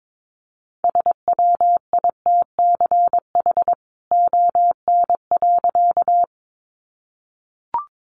Morse code, heard as a keyed tone, about 22 wpm, 700 Hz. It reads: SWITCH ON.
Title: A Sound Of Thunder 22wpm